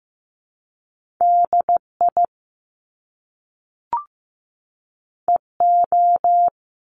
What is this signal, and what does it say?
Morse code, heard as a keyed tone, about 15 wpm, 700 Hz. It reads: DI EO
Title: Top Two Letter Combinations in Words 15wpm